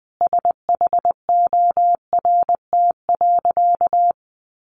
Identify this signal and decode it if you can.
Morse code, heard as a keyed tone, about 20 wpm, 700 Hz. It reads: SHORT.
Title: Sentences from Top 500 Words 20wpm